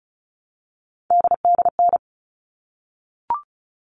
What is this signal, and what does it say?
Morse code, heard as a keyed tone, about 34 wpm, 700 Hz. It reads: DDN
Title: All Cut Numbers - 3-Digit 35wpm